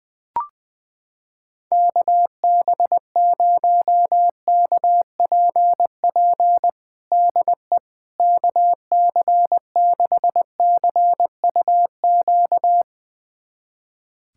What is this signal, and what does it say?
Morse code, heard as a keyed tone, about 20 wpm, 700 Hz. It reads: KB0KPP DE KC6CUQ
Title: USA Callsign DE Callsign 20wpm